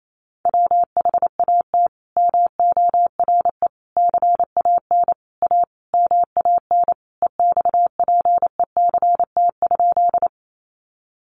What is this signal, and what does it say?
Morse code, heard as a keyed tone, about 28 wpm, 700 Hz. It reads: WHAT MORE CAN A MAN EXPECT?
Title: Sentences from Top 1100 Words 28wpm